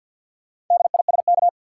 Morse code, heard as a keyed tone, about 52 wpm, 700 Hz. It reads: DISK
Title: Sentences from Top 1000 Words - Review Part 6 - New Word Drill - 50wpm